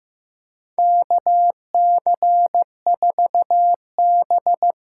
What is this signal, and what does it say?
Morse code, heard as a keyed tone, about 15 wpm, 700 Hz. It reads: KC4B
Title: USA Call Signs - 6 Characters - 8x Character Spacing - 15wpm